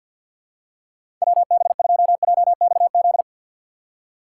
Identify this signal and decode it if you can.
Morse code, heard as a keyed tone, about 50 wpm, 700 Hz. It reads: WB2JXZ
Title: USA Call Signs - 6 Characters - 6x Character Spacing - 50wpm